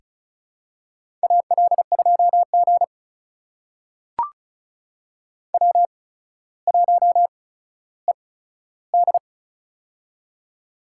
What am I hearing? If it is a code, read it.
Morse code, heard as a keyed tone, about 36 wpm, spaced as for 15 wpm, 700 Hz. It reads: AL2G W1ED
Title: USA Call Signs - 4 Characters - 8x Character Spacing - 35wpm